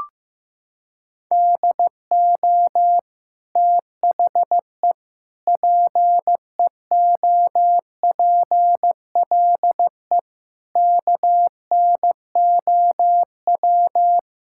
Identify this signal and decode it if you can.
Morse code, heard as a keyed tone, about 16 wpm, 700 Hz. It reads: DO THE PEOPLE KNOW
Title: Sentences from Top 1000 Words - Review Part 5 - 15wpm